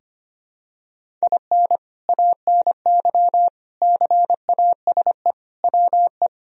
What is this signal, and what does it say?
Morse code, heard as a keyed tone, about 25 wpm, 700 Hz. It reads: IN ANY CASE WE
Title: Sentences from Top 1000 Words - Review Part 1 - 25wpm